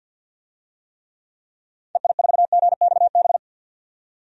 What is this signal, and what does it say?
Morse code, heard as a keyed tone, about 48 wpm, 700 Hz. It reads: EI4GXB
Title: DXCC-100 Call Signs - 6 Characters - 50wpm